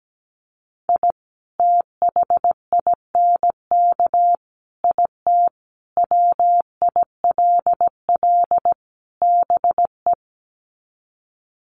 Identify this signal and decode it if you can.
Morse code, heard as a keyed tone, about 17 wpm, 700 Hz. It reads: I THINK IT WILL BE
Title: Sets of 5 Words Encore2 17wpm